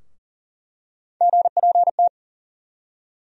Morse code, heard as a keyed tone, about 40 wpm, 700 Hz. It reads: GPT
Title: USA Call Sign Postfixes - 3 Letters 40wpm